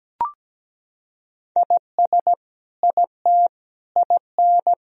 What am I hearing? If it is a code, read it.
Morse code, heard as a keyed tone, about 17 wpm, 700 Hz. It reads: IS IT IN
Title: Sentences from Top 1000 Words - Review Part 4 - 17wpm